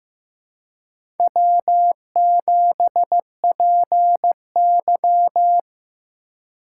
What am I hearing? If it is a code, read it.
Morse code, heard as a keyed tone, about 15 wpm, 700 Hz. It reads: W7PY
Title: USA Call Signs - 4 Characters - 3x Character Spacing - 15wpm